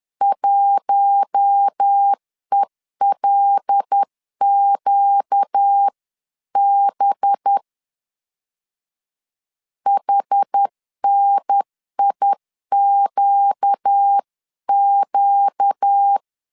Morse code, heard as a keyed tone, about 10 wpm, 800 Hz. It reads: 1ELQB HNIQQ
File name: CW / DET / 1 6 1500C S C 10wpm 4252